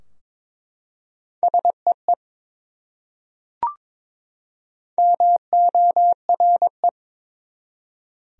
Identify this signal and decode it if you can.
Morse code, heard as a keyed tone, about 22 wpm, 700 Hz. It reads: SEE MORE
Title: Top 100 Words 22wpm